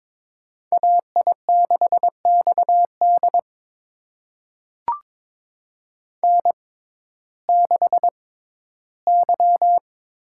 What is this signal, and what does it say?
Morse code, heard as a keyed tone, about 22 wpm, spaced as for 13 wpm, 700 Hz. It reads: AI6XD N6Y
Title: USA Call Signs - 5 Characters - 6x Character Spacing - 22wpm